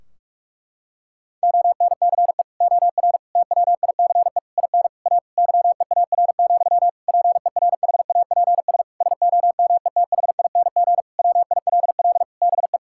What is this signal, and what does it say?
Morse code, heard as a keyed tone, about 46 wpm, 700 Hz. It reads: ONCE OR TWICE IN A YEAR, PERHAPS SOMETHING WILL BE
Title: Sentences from Top 1600 Words 45wpm